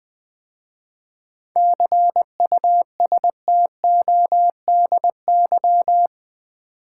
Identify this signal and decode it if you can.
Morse code, heard as a keyed tone, about 20 wpm, 700 Hz. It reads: CUSTODY
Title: 7 Letter Words - First 3 Letters Given - 20wpm